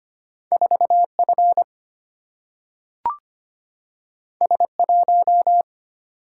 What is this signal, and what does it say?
Morse code, heard as a keyed tone, about 25 wpm, 700 Hz. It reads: 4F S1
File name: Two Letter-Number Combinations - 25wpm